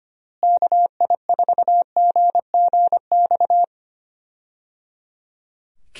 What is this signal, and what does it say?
Morse code, heard as a keyed tone, about 25 wpm, 700 Hz. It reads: KI4GGX
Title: USA Call Signs - 6 Characters - Repeated 3x - 25wpm